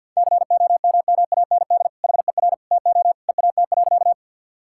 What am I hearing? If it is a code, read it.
Morse code, heard as a keyed tone, about 48 wpm, 700 Hz. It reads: COMMAND HER TO EAT.
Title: Sentences from Top 1600 Words 50wpm